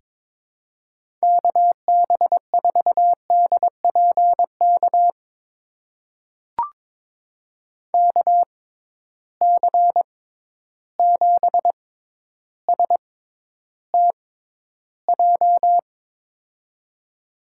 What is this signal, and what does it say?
Morse code, heard as a keyed tone, about 22 wpm, spaced as for 11 wpm, 700 Hz. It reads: KB4DPK KC7STJ
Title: USA Call Signs - 6 Characters - 6x Character Spacing - 22wpm